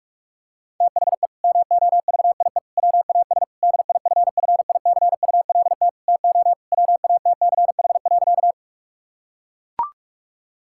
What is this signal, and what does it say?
Morse code, heard as a keyed tone, about 41 wpm, 700 Hz. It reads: THE MOVIE WAS DIFFICULT TO WATCH.
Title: Sentences from Top 1000 Words - Review Part 5 - 45wpm